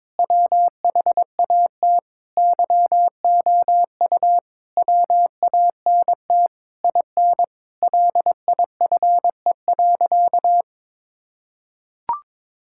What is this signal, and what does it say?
Morse code, heard as a keyed tone, about 22 wpm, 700 Hz. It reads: WHAT YOU WANT IN LIFE.
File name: Sentences with 7-Word Phrases - 22wpm